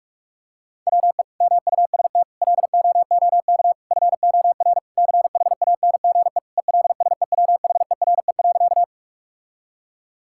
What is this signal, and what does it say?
Morse code, heard as a keyed tone, about 46 wpm, 700 Hz. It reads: WE MUST LOOK FOR CHANGE ELSEWHERE.
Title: Sentences from Top 1600 Words 45wpm